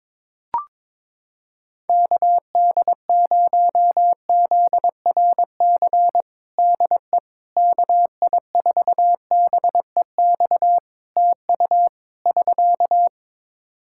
Morse code, heard as a keyed tone, about 22 wpm, 700 Hz. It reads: KD0ZRC DE KI4BEX TU <SK>
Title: Practice QSO - Exchange 4 - Ending - 22wpm